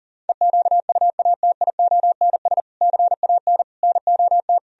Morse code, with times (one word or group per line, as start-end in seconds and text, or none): 0.3-2.6 EQUATIONS
2.8-3.6 CAN
3.8-4.6 NOT